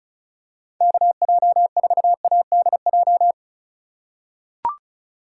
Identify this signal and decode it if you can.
Morse code, heard as a keyed tone, about 34 wpm, 700 Hz. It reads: KJ4ADJ
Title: USA Call Signs - 6 Characters - 2x Character Spacing - 35wpm